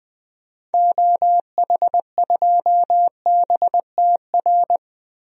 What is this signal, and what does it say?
Morse code, heard as a keyed tone, about 20 wpm, 700 Hz. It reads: OH2BTR
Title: DXCC-100 Call Signs - 6 Characters - 20wpm